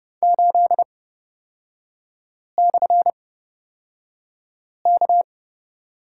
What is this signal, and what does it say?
Morse code, heard as a keyed tone, about 30 wpm, spaced as for 5 wpm, 690 Hz. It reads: 8/K
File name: Lesson 37e - Letter x - Three Characters 30wpm